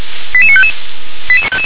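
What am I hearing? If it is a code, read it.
Five-tone selective call (ZVEI), heard as quick runs of five tones, tones of about 70 ms. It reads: 88955
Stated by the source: Received on an SDR; ZVEI five-tone selective call.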